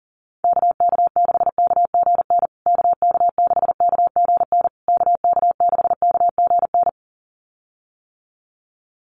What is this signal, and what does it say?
Morse code, heard as a keyed tone, about 40 wpm, 700 Hz. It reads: KK6KGN KK6KGN KK6KGN
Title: USA Call Signs Repeated 3x 40wpm